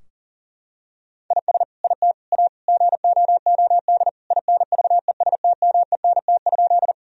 Morse code, heard as a keyed tone, about 40 wpm, 700 Hz. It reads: IS IT A GOOD INVESTMENT?
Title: Sentences from Top 1000 Words - Review Part 4 - 40wpm